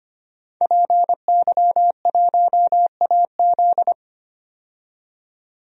Morse code, heard as a keyed tone, about 25 wpm, 700 Hz. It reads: PY1AZ
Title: DXCC-100 Call Signs - 5 Characters - 25wpm